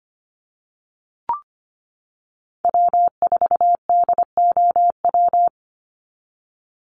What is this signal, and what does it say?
Morse code, heard as a keyed tone, about 25 wpm, 700 Hz. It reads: W4DOW
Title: NA SOTA Chaser Callsigns 25wpm